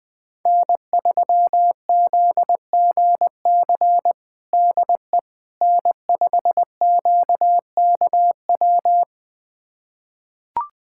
Morse code, heard as a keyed tone, about 21 wpm, 700 Hz. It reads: N3ZGC DE N5QKW
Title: USA Callsign DE Callsign - 5 Characters - 20wpm